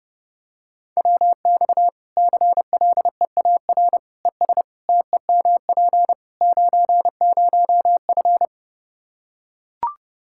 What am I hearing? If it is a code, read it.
Morse code, heard as a keyed tone, about 30 wpm, 700 Hz. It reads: WX CLEAR ES TEMP 90F
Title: Practice QSO - Exchange 2 - Info - 30wpm